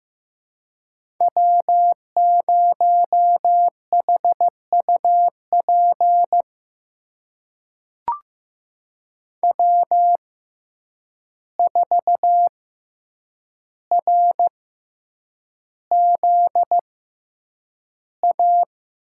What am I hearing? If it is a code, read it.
Morse code, heard as a keyed tone, about 15 wpm, spaced as for 8 wpm, 700 Hz. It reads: W0HUP W4RZA
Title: USA Call Signs - 5 Characters - 6x Character Spacing - 15wpm